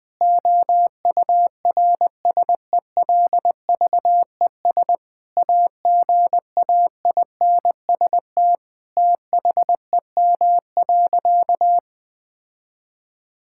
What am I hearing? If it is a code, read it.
Morse code, heard as a keyed tone, about 21 wpm, 700 Hz. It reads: OURSELVES AGAINST THEM.
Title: Sentences from Top 1600 Words 20wpm